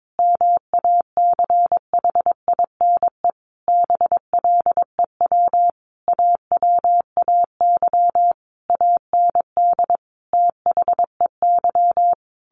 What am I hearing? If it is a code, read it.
Morse code, heard as a keyed tone, about 22 wpm, 700 Hz. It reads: MACHINE BLEW AWAY AND THEY
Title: A Sound Of Thunder 22wpm